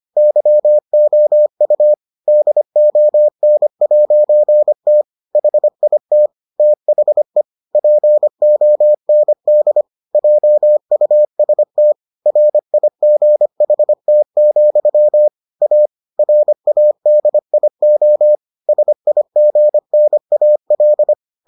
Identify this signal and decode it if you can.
Morse code, heard as a keyed tone, about 25 wpm, 600 Hz. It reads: YOU DON'T HIT THE POND JUST RIGHT, A RADIO SIGNAL